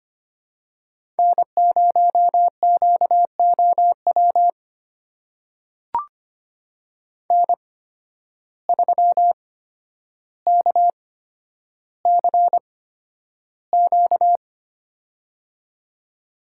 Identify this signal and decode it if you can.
Morse code, heard as a keyed tone, about 25 wpm, spaced as for 11 wpm, 700 Hz. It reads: N0QOW N3KCQ
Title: USA Call Signs - 5 Characters - 8x Character Spacing - 25wpm